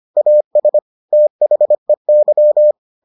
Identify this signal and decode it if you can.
Morse code, heard as a keyed tone, about 25 wpm, 600 Hz. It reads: AS THEY